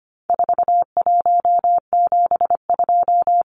Morse code, heard as a keyed tone, about 25 wpm, 700 Hz. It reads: 4172
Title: Numbers - 4-Digit 25wpm